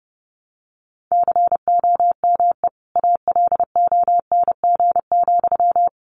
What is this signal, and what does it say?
Morse code, heard as a keyed tone, about 30 wpm, 700 Hz. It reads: COME ALONG,
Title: Sentences from Top 700 Words 30wpm